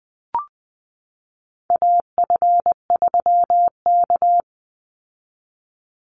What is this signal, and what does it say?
Morse code, heard as a keyed tone, about 20 wpm, 700 Hz. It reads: AF3K
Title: Call Signs Repeated 2x - 20wpm